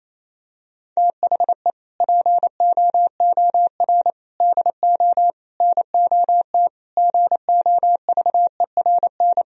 Morse code, heard as a keyed tone, about 28 wpm, 700 Hz. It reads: THE POOR DO NOT GOVERN
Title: Sentences from Top 1000 Words - Review Part 2 - 28wpm